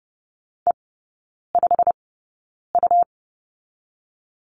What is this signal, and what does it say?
Morse code, heard as a keyed tone, about 30 wpm, spaced as for 9 wpm, 690 Hz. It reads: E5U
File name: Lesson 17e - Letter u - Three Characters 30wpm